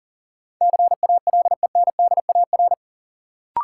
Morse code, heard as a keyed tone, about 38 wpm, 700 Hz. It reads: CALENDAR
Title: Sentences from Top 1600 Words - New Word Drill - 40wpm